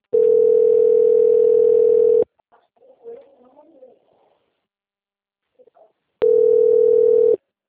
Ringback tone (North America), 440 + 480 Hz.